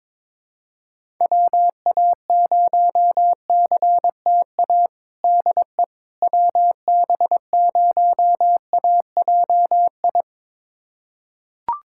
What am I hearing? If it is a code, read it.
Morse code, heard as a keyed tone, about 22 wpm, 700 Hz. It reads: WA0CTA DE WB0AJI
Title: USA Callsign DE Callsign - 6 Characters - 22wpm